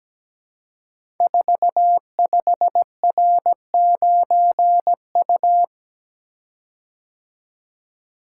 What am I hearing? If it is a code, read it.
Morse code, heard as a keyed tone, about 17 wpm, 700 Hz. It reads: 45R9U
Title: Code Groups - 5 Letter:Numbers - 17wpm